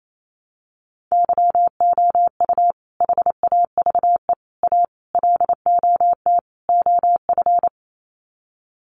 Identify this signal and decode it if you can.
Morse code, heard as a keyed tone, about 28 wpm, 700 Hz. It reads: YOU HAVE A LOT OF
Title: Sets of 5 Words Encore 28wpm